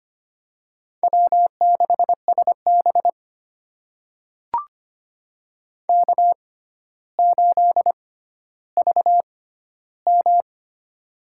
Morse code, heard as a keyed tone, about 25 wpm, spaced as for 13 wpm, 700 Hz. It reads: W6SB K8VM
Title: USA Call Signs - 4 Characters - 6x Character Spacing - 25wpm